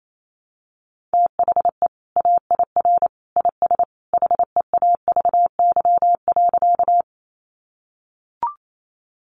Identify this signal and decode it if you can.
Morse code, heard as a keyed tone, about 28 wpm, 700 Hz. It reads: THE AIR IS HEAVY.
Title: Sentences from Top 1100 Words 28wpm